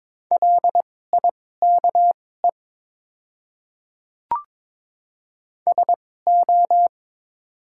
Morse code, heard as a keyed tone, about 22 wpm, spaced as for 16 wpm, 690 Hz. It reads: LIKE SO
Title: 2x Character Spacing - Top 100 Words - 22wpm